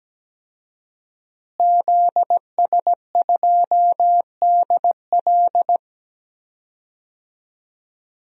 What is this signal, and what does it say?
Morse code, heard as a keyed tone, about 17 wpm, 700 Hz. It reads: ZS2DL
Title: DXCC-100 Call Signs - 5 Characters - Repeated 2x - 17wpm